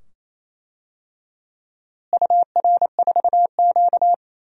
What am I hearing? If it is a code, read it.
Morse code, heard as a keyed tone, about 28 wpm, 700 Hz. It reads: UR4Q
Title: DXCC-100 Call Signs - 4 Characters - Repeated 2x - 28wpm